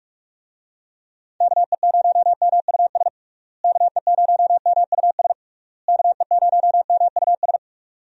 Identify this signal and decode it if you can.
Morse code, heard as a keyed tone, about 42 wpm, 700 Hz. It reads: KE0MUS KE0MUS KE0MUS
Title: USA Call Signs - 6 Characters - Repeated 3x - 45wpm